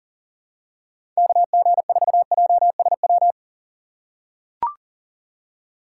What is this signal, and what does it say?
Morse code, heard as a keyed tone, about 40 wpm, 700 Hz. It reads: KG4JSW
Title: USA Call Signs - 6 Characters - 2x Character Spacing - 40wpm